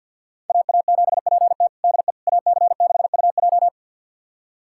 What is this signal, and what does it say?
Morse code, heard as a keyed tone, about 51 wpm, 700 Hz. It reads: AA7PT DE AC6UJ
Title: USA Callsign DE Callsign - 5 Characters - 50wpm